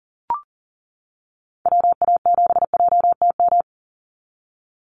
Morse code, heard as a keyed tone, about 40 wpm, 700 Hz. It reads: WA7JTM
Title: NA SOTA Activator Callsigns 40wpm